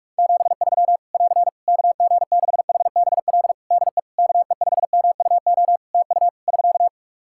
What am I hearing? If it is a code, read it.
Morse code, heard as a keyed tone, about 47 wpm, 700 Hz. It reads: 73 <AR> KG6HBL DE KE5MUO TU <SK>
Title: Practice QSO - Exchange 4 - Ending - 45wpm